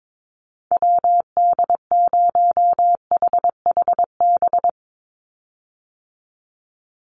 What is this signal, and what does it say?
Morse code, heard as a keyed tone, about 22 wpm, 700 Hz. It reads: WD0HHB
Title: USA Call Signs 22wpm